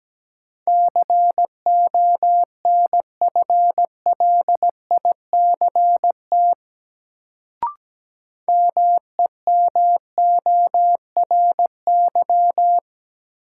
Morse code, heard as a keyed tone, about 17 wpm, 700 Hz. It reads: CONFLICT MEMORY
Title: Sentences from Top 1300 Words - New Word Drill - 17wpm